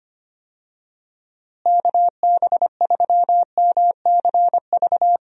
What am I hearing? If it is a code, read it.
Morse code, heard as a keyed tone, about 25 wpm, 700 Hz. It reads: KB3MCV
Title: USA Call Signs - 6 Characters - Repeated 2x - 25wpm